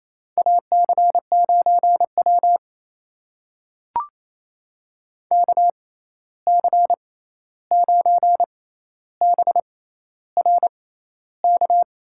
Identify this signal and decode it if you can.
Morse code, heard as a keyed tone, about 29 wpm, spaced as for 13 wpm, 700 Hz. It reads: AC9W KC9BRK
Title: USA Call Signs - 6x Character Spacing - 28wpm